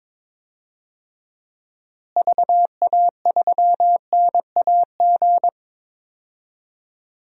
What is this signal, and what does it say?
Morse code, heard as a keyed tone, about 21 wpm, 700 Hz. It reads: VA3NAG
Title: DXCC-100 Call Signs - 6 Characters - Repeated 3x - 22wpm